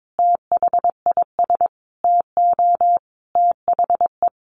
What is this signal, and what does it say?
Morse code, heard as a keyed tone, about 22 wpm, 700 Hz. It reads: THIS TO THE
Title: Sentences from Top 200 Words 22wpm